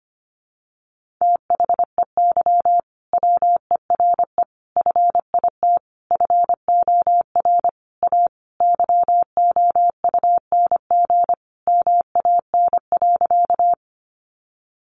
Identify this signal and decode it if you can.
Morse code, heard as a keyed tone, about 25 wpm, 700 Hz. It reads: THEY WERE FIT FOR A YOUNG MAN.
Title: Sentences from Top 800 Words 25wpm